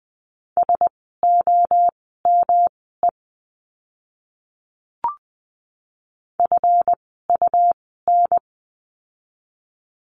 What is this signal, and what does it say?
Morse code, heard as a keyed tone, about 20 wpm, spaced as for 14 wpm, 700 Hz. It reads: SOME FUN
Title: 139 Lesson 23f - Letter f - Words 20wpm